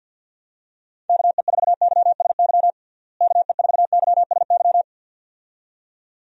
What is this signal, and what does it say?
Morse code, heard as a keyed tone, about 51 wpm, 700 Hz. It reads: KE4YSY KE4YSY
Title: USA Call Signs - 6 Characters - Repeated 2x - 50wpm